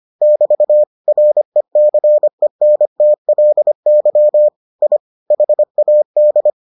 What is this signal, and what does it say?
Morse code, heard as a keyed tone, about 25 wpm, 600 Hz. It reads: <BT> RECENTLY I HAD